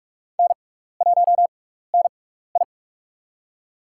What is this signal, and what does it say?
Morse code, heard as a keyed tone, about 45 wpm, spaced as for 15 wpm, 700 Hz. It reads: N1NI
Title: USA Call Signs - 4 Characters - 6x Character Spacing - 45wpm